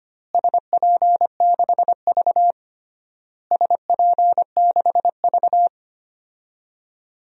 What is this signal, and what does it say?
Morse code, heard as a keyed tone, about 25 wpm, 700 Hz. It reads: SP6V SP6V
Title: DXCC-100 Call Signs - 4 Characters - Repeated 3x - 25wpm